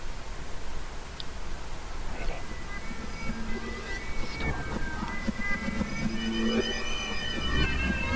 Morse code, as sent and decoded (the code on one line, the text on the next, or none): none